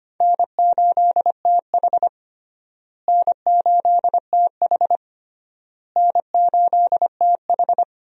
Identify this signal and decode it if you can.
Morse code, heard as a keyed tone, about 25 wpm, 700 Hz. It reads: N8TH N8TH N8TH
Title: USA Call Signs - 4 Characters - Repeated 3x - 25wpm